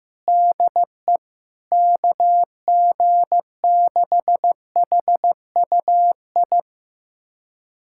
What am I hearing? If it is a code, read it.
Morse code, heard as a keyed tone, about 15 wpm, 700 Hz. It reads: DE KG6HUI
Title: USA Callsign DE Callsign 15wpm